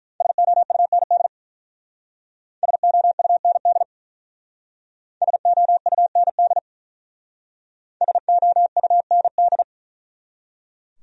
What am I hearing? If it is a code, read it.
Morse code, heard as a keyed tone, about 42 wpm, 700 Hz. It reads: SOUND SOUND SOUND SOUND
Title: Speed-Racing Top 100 Words 35wpm